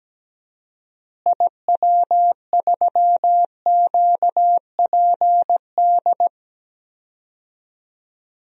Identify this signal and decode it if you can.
Morse code, heard as a keyed tone, about 17 wpm, 700 Hz. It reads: IW3QPD
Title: DXCC-100 Call Signs - 6 Characters - Repeated 3x - 17wpm